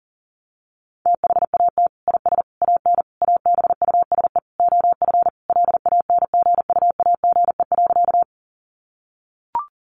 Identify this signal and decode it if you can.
Morse code, heard as a keyed tone, about 40 wpm, 700 Hz. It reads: THAT IS AN ABUSE OF LANGUAGE.
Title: Sentences from Top 1300 Words 40wpm